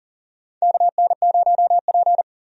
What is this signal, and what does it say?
Morse code, heard as a keyed tone, about 40 wpm, 700 Hz. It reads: KN0P
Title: USA Call Signs - 4 Characters - 40wpm